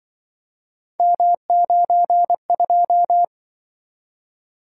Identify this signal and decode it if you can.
Morse code, heard as a keyed tone, about 24 wpm, 700 Hz. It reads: M92
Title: Speed-Racing - Three Letter-Number Combinations - 20wpm